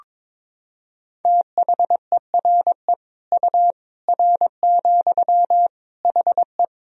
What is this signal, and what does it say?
Morse code, heard as a keyed tone, about 22 wpm, 700 Hz. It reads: THERE U R, HE
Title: Sentences from Top 100 Words and Abbreviations 22wpm